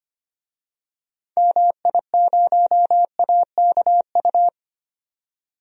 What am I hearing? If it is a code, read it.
Morse code, heard as a keyed tone, about 25 wpm, 700 Hz. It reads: MI0AKU
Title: DXCC-100 Call Signs - 6 Characters - 25wpm